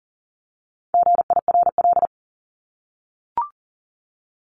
Morse code, heard as a keyed tone, about 40 wpm, 700 Hz. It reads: GIRL
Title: Top 500 Words 40wpm